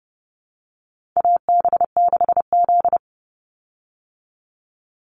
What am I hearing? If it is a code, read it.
Morse code, heard as a keyed tone, about 30 wpm, 700 Hz. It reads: AB6Z
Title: Call Signs Repeated 3x - 30wpm